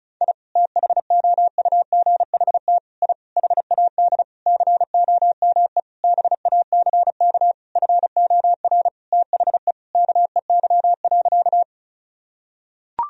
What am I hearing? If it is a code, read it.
Morse code, heard as a keyed tone, about 36 wpm, 700 Hz. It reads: I THOUGHT I HAD COME BACK FOR THE KEY.
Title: Sentences from Top 1000 Words - Review Part 4 - 35wpm